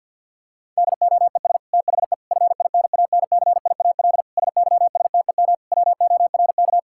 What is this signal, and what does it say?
Morse code, heard as a keyed tone, about 52 wpm, 690 Hz. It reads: DOES THE FINANCIAL SYSTEM WORK
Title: Sentences from Top 1000 Words - Review Part 5 - 50wpm